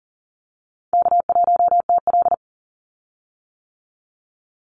Morse code, heard as a keyed tone, about 41 wpm, 700 Hz. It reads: K1TL
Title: Call Signs - 40wpm